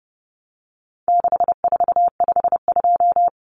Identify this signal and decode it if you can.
Morse code, heard as a keyed tone, about 30 wpm, 700 Hz. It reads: 6452
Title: Common Cut Numbers - 4-Digit 30wpm